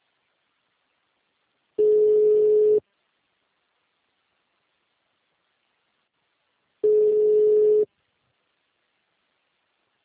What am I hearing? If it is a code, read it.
Ringback tone (Europe, 425 Hz).
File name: out- -2013- - - .59